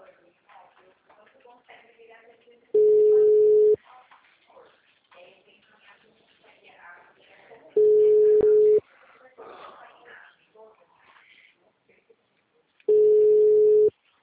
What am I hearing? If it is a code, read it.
Ringback tone (Europe, 425 Hz), with speech.
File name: out- -2020- - - .114